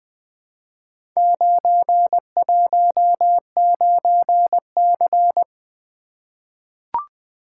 Morse code, heard as a keyed tone, about 20 wpm, 700 Hz. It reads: 919C
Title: Four Letter-Number Combinations - 20wpm